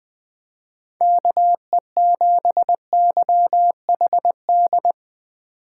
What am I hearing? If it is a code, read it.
Morse code, heard as a keyed tone, about 20 wpm, 700 Hz. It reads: KE7YHD
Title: Speed Racing - USA 2x3 Call Signs 17wpm